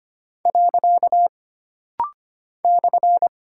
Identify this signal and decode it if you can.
Morse code, heard as a keyed tone, about 25 wpm, 700 Hz. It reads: . /
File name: Single Punctuation-Prosign - Rapid-Fire - 25wpm